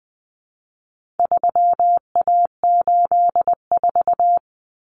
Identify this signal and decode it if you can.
Morse code, heard as a keyed tone, about 20 wpm, 700 Hz. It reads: 3A84
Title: Common Cut Numbers - 4-Digit 20wpm